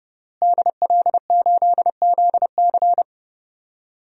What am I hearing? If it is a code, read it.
Morse code, heard as a keyed tone, about 30 wpm, 700 Hz. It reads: DL8ZC
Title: DXCC-100 Call Signs - 5 Characters - Repeated 3x - 30wpm